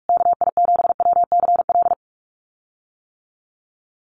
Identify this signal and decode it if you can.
Morse code, heard as a keyed tone, about 44 wpm, 700 Hz. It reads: KI7WCL
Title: USA Call Signs Repeated 3x 45wpm